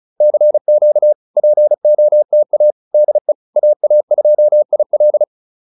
Morse code, heard as a keyed tone, about 35 wpm, 590 Hz. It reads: CQ POTA DE AA2IL